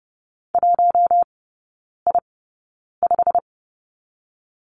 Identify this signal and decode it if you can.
Morse code, heard as a keyed tone, about 30 wpm, spaced as for 9 wpm, 700 Hz. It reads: 1I5
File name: Lesson 18e - Letter m - Three Characters 30wpm